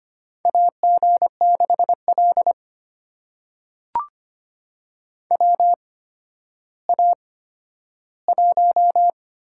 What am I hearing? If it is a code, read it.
Morse code, heard as a keyed tone, about 25 wpm, spaced as for 12 wpm, 700 Hz. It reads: AG6L WA1
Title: USA Call Signs - 8x Character Spacing - 25wpm